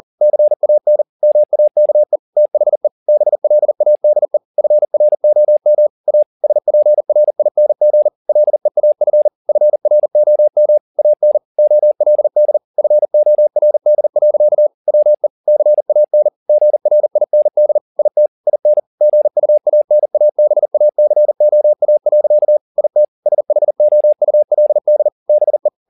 Morse code, heard as a keyed tone, about 40 wpm, 600 Hz. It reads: CAN MAKE THE BLADE FROM A SPRING LEAF FROM AN OLD FORD. WE CAN GRIND IT IN GUANABACOA. IT SHOULD BE
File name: the old man and the sea 2 40wpm